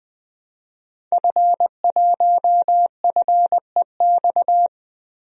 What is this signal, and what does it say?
Morse code, heard as a keyed tone, about 20 wpm, 700 Hz. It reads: F1FEX
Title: DXCC-100 Call Signs - 5 Characters - Repeated 2x - 20wpm